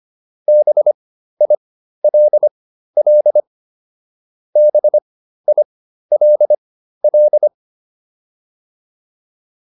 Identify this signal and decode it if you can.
Morse code, heard as a keyed tone, about 25 wpm, spaced as for 13 wpm, 600 Hz. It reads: BILL BILL
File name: QSO401 13